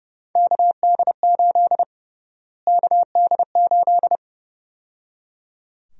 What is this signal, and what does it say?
Morse code, heard as a keyed tone, about 30 wpm, 700 Hz. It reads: KD8 KD8
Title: USA Call Sign Prefixes - Repeated 2x - 30wpm